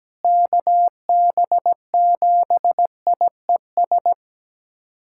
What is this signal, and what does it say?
Morse code, heard as a keyed tone, about 17 wpm, 700 Hz. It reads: KB7IES
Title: USA Callsign DE Callsign - 6 Characters - 17wpm